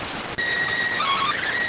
SSTV (slow-scan television), PD 290.